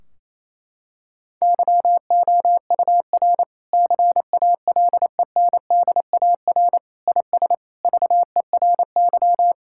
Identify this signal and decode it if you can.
Morse code, heard as a keyed tone, about 28 wpm, 700 Hz. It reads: YOUR CALENDAR IS VERY
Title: Sentences from Top 1600 Words 28wpm